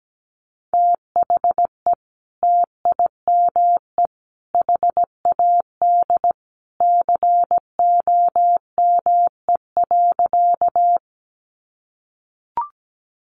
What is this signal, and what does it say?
Morse code, heard as a keyed tone, about 17 wpm, 700 Hz. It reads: THE TIME HAD COME.
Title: Sentences from Top 100 Words 17wpm